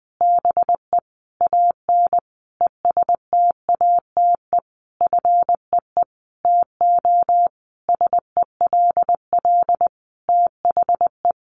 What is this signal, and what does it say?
Morse code, heard as a keyed tone, about 20 wpm, 700 Hz. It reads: BE AN ESTATE FEE TO SELL THE
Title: Sentences from Top 1500 Words 20wpm